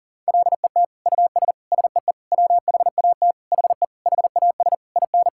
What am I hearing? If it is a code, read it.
Morse code, heard as a keyed tone, about 40 wpm, 700 Hz. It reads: LET US SEE WHAT HE HAS IN
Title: Sentences from Top 1000 Words - Review Part 5 - 40wpm